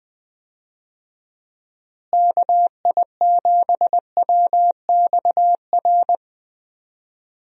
Morse code, heard as a keyed tone, about 21 wpm, 700 Hz. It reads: KI7WXR
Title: USA Call Signs - 6 Characters - Repeated 3x - 20wpm